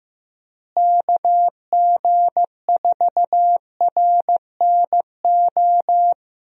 Morse code, heard as a keyed tone, about 15 wpm, 700 Hz. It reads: KG4RNO
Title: USA Call Signs - 6 Characters - Repeated 3x - 15wpm